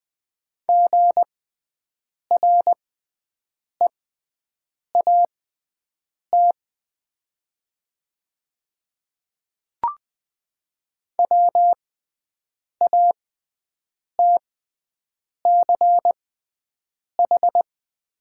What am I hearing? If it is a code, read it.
Morse code, heard as a keyed tone, about 20 wpm, spaced as for 7 wpm, 700 Hz. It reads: GREAT WATCH
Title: 6x Character Spacing - Top 300 Words - 20wpm